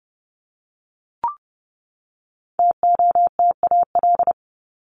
Morse code, heard as a keyed tone, about 30 wpm, 700 Hz. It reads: TOTAL
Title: LY Suffix - Top 100 Words with Base Words 30wpm